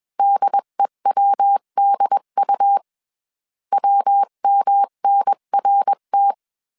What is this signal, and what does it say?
Morse code, heard as a keyed tone, about 21 wpm, 800 Hz. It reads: DEWDU WMNRT
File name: 20wpm